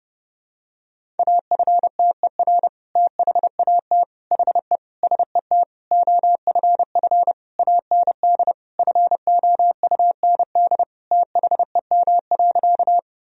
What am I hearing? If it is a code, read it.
Morse code, heard as a keyed tone, about 30 wpm, 700 Hz. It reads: AFTER THAT HE SET OFF AND FOUND THEM.
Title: Sentences from Top 1000 Words - Review Part 1 - 30wpm